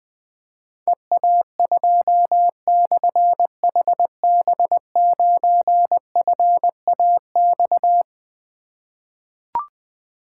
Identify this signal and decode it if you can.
Morse code, heard as a keyed tone, about 20 wpm, 700 Hz. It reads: EA2/HB9FAX
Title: Difficult Call Signs 20wpm